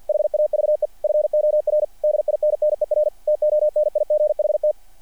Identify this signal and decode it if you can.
Morse code, heard as a keyed tone, about 48 wpm, 600 Hz. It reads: HAVE FOR DINNER TONIGHT
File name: archived / CRSnet